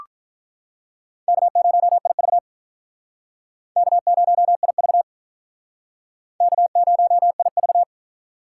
Morse code, heard as a keyed tone, about 44 wpm, 700 Hz. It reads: K0IV K0IV K0IV